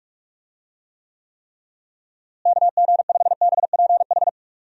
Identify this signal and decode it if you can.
Morse code, heard as a keyed tone, about 46 wpm, 700 Hz. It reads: KG5BPH
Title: USA Call Signs - 6 Characters - Repeated 3x - 45wpm